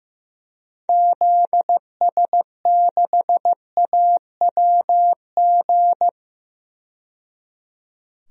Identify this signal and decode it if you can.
Morse code, heard as a keyed tone, about 15 wpm, 700 Hz. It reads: ZS6AWG
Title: DXCC-100 Call Signs - 6 Characters - 15wpm